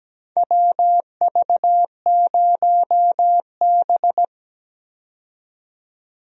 Morse code, heard as a keyed tone, about 17 wpm, 700 Hz. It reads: WV0B
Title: USA Call Signs - 4 Characters - 17wpm